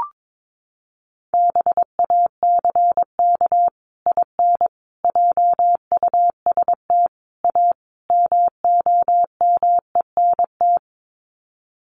Morse code, heard as a keyed tone, about 22 wpm, 700 Hz. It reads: BACK IN JUST A MOMENT
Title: Sets of 5 Words Encore 22wpm